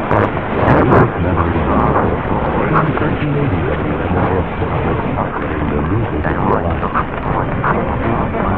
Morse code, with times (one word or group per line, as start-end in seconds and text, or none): none